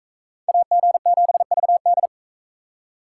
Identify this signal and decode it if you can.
Morse code, heard as a keyed tone, about 40 wpm, 700 Hz. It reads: AG7VD